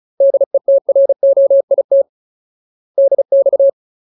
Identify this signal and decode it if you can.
Morse code, heard as a keyed tone, about 34 wpm, 550 Hz. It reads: DETROIT DX
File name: Ultimate Word List / 35WPM Ultimate